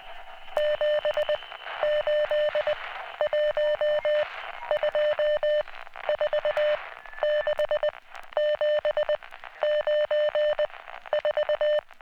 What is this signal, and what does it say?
Morse code, heard as a keyed tone, about 20 wpm, 600 Hz. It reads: 7 8 1 2 4 6 7 9 4